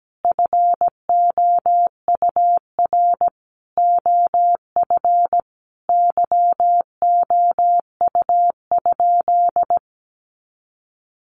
Morse code, heard as a keyed tone, about 17 wpm, 700 Hz. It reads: FOUR OF YOU?
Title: Sentences from Top 300 Words 17wpm